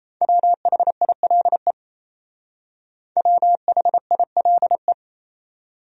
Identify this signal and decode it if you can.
Morse code, heard as a keyed tone, about 35 wpm, 700 Hz. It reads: WHILE WHILE
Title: Speed-Racing Top 500 Words 28wpm